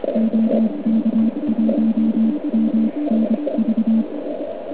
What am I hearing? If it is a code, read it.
Morse code, heard as a keyed tone, about 24 wpm, spaced as for 21 wpm, 220 Hz. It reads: OK1MNV